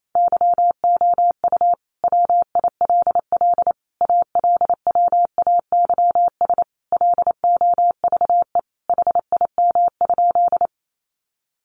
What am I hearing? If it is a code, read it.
Morse code, heard as a keyed tone, about 28 wpm, 700 Hz. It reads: YOU WILL ALWAYS LOVE HIM?
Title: Sentences from Top 400 Words 28wpm